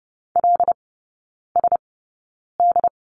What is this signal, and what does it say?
Morse code, heard as a keyed tone, about 30 wpm, spaced as for 9 wpm, 700 Hz. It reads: LSD
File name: 081 Lesson 14e - Number 2 - Three Characters 30wpm